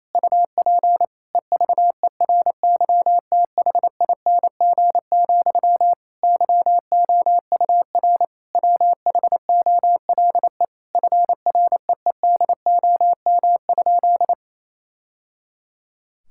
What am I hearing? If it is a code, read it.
Morse code, heard as a keyed tone, about 28 wpm, 700 Hz. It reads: UP EVERYTHING, YOUR WHOLE FREEDOM?
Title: Sentences from Top 1000 Words - Review Part 5 - 28wpm